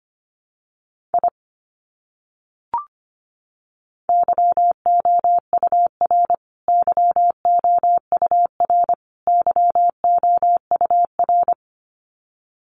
Morse code, heard as a keyed tone, about 25 wpm, 700 Hz. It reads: I YOUR YOUR YOUR
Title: Top 200 Words Repeated 25wpm